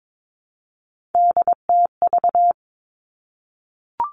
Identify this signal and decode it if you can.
Morse code, heard as a keyed tone, about 22 wpm, 700 Hz. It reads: DTV
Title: Three Challenging Letter-Number Combinations 22wpm